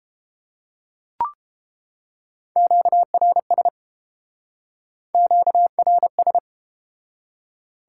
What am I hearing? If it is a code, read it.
Morse code, heard as a keyed tone, about 29 wpm, 700 Hz. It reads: QRS QRS